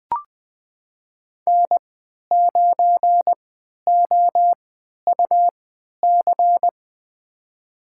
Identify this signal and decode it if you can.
Morse code, heard as a keyed tone, about 21 wpm, spaced as for 12 wpm, 700 Hz. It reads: N9OUC
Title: USA Call Signs - 5 Characters - 3x Character Spacing - 20wpm